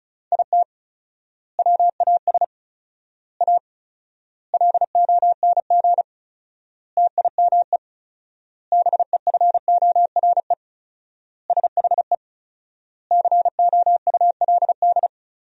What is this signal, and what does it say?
Morse code, heard as a keyed tone, about 36 wpm, 700 Hz. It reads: IT WAS A LONG TIME BEFORE SHE COULD
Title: 4x Word-Spacing - Sentences from Top 500 Words - 35wpm